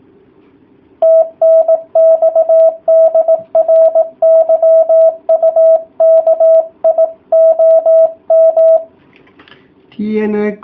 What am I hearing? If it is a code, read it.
Morse code, heard as a keyed tone, about 18 wpm, 650 Hz. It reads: TNX DR YUKI OM